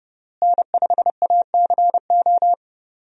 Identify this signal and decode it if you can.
Morse code, heard as a keyed tone, about 30 wpm, 700 Hz. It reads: N5ACO
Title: USA Call Signs - 5 Characters - Repeated 3x - 30wpm